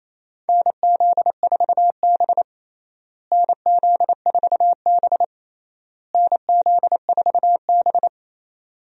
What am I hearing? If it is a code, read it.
Morse code, heard as a keyed tone, about 28 wpm, 700 Hz. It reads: NZ4B NZ4B NZ4B
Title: USA Call Signs - 4 Characters - Repeated 3x - 28wpm